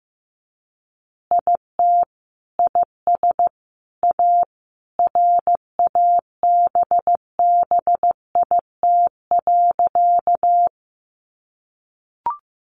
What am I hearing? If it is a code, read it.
Morse code, heard as a keyed tone, about 15 wpm, 700 Hz. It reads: IT IS A RABBIT.
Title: Easy Phrases 15wpm